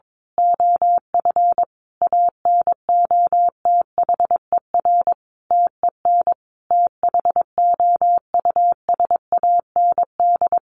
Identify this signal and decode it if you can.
Morse code, heard as a keyed tone, about 22 wpm, 700 Hz. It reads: OF ANOTHER TEN THOUSAND
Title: A Sound Of Thunder 22wpm